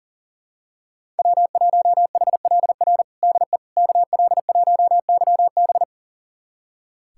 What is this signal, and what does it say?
Morse code, heard as a keyed tone, about 40 wpm, 700 Hz. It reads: W1HLR DE KL1YB
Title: USA Callsign DE Callsign - 5 Characters - 40wpm